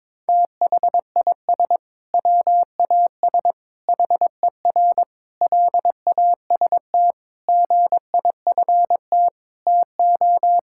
Morse code, heard as a keyed tone, about 22 wpm, 700 Hz. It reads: THIS WAS HER LAST GIFT TO
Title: Sentences from Top 1000 Words - Review Part 6 - 22wpm